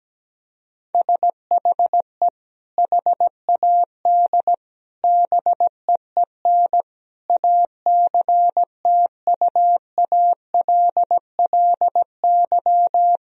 Morse code, heard as a keyed tone, about 17 wpm, 700 Hz. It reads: SHE HAD BEEN ACTUALLY
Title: Sentences from Top 1000 Words - Review Part 4 - 17wpm